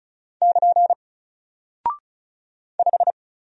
Morse code, heard as a keyed tone, about 36 wpm, 700 Hz. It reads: <KN> 5
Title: Single Character (Letter:Number:Punctuation:Prosign) Alt 35wpm